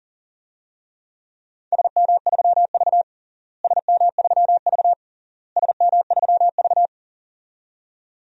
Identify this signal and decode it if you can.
Morse code, heard as a keyed tone, about 40 wpm, 700 Hz. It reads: SM3V SM3V SM3V
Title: DXCC-100 Call Signs - 4 Characters - Repeated 3x - 40wpm